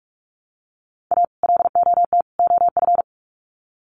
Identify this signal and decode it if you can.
Morse code, heard as a keyed tone, about 44 wpm, 700 Hz. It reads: A LOT OF
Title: Sets of 3 Words 45wpm